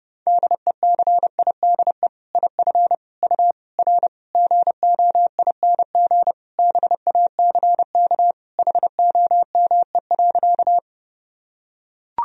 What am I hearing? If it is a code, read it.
Morse code, heard as a keyed tone, about 32 wpm, 700 Hz. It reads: DECIDE IF U R GOING BACK HOME.
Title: Sentences from Top 500 Words and Abbreviations 30wpm